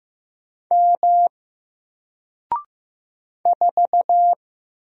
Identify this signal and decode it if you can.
Morse code, heard as a keyed tone, about 15 wpm, 700 Hz. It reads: M 4
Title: Single Letter-Number 15wpm